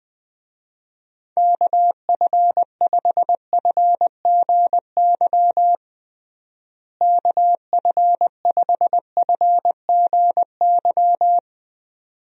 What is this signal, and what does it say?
Morse code, heard as a keyed tone, about 20 wpm, 700 Hz. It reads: KF5FGY KF5FGY
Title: USA Call Signs - 6 Characters - Repeated 2x - 20wpm